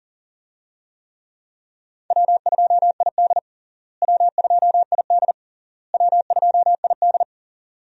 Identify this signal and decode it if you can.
Morse code, heard as a keyed tone, about 40 wpm, 700 Hz. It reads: W2ID W2ID W2ID
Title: USA Call Signs - 4 Characters - Repeated 3x - 40wpm